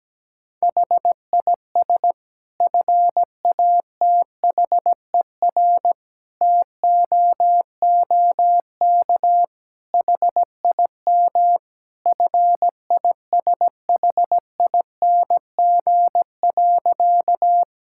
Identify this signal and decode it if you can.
Morse code, heard as a keyed tone, about 17 wpm, 700 Hz. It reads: HIS FATHER TOOK HIM FISHING.
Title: Sentences from Top 1000 Words - Review Part 5 - 17wpm